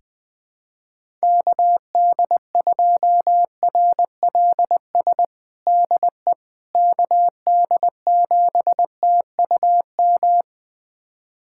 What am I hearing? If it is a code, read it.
Morse code, heard as a keyed tone, about 20 wpm, 700 Hz. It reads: KD2RLS DE KD7TUM
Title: USA Callsign DE Callsign - 6 Characters - 20wpm